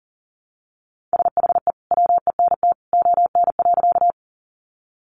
Morse code, heard as a keyed tone, about 40 wpm, 700 Hz. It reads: SHE WENT ON.
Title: Sentences from Top 200 Words 40wpm